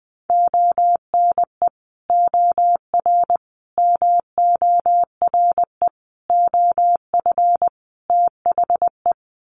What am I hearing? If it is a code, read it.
Morse code, heard as a keyed tone, about 20 wpm, 700 Hz. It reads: ONE OR MORE OF THE
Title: Sets of 5 Words Encore 20wpm